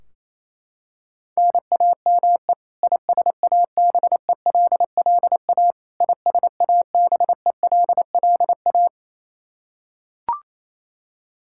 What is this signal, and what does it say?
Morse code, heard as a keyed tone, about 27 wpm, 700 Hz. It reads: NAME ISABELLA ISABELLA
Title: K1USN SST QSO Practice 28wpm